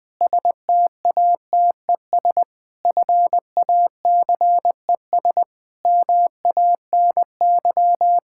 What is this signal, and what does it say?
Morse code, heard as a keyed tone, about 20 wpm, 700 Hz. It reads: STATES FACES MANY
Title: Sentences with 7-Word Phrases - 20wpm